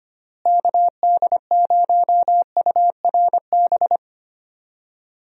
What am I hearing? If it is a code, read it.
Morse code, heard as a keyed tone, about 25 wpm, 700 Hz. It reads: KD0URB
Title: USA Call Signs - 6 Characters - Repeated 3x - 25wpm